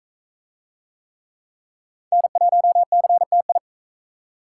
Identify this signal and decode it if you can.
Morse code, heard as a keyed tone, about 44 wpm, 700 Hz. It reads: N1CTI